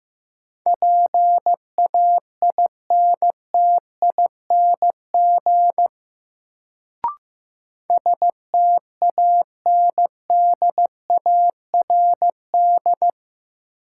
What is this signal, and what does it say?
Morse code, heard as a keyed tone, about 15 wpm, 700 Hz. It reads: PAINTING STANDARD
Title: Sentences from Top 1000 Words - Review Part 6 - New Word Drill - 15wpm